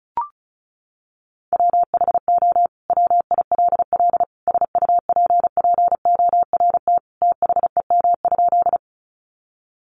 Morse code, heard as a keyed tone, about 34 wpm, 700 Hz. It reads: WHO WILL SUPPORT THEM?
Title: Sentences from Top 800 Words 35wpm